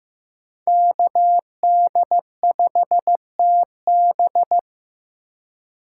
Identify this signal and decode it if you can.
Morse code, heard as a keyed tone, about 15 wpm, 700 Hz. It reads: KD5TB
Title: USA Call Signs - 5 Characters - 2x Character Spacing - 15wpm